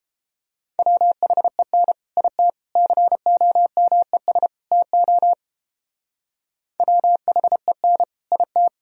Morse code, heard as a keyed tone, about 31 wpm, 700 Hz. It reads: WHEN IT COMES TO WHEN IT